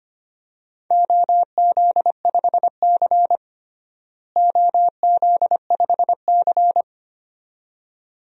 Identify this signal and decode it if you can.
Morse code, heard as a keyed tone, about 25 wpm, 700 Hz. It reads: OZ5C OZ5C
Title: DXCC-100 Call Signs - 4 Characters - Repeated 2x - 25wpm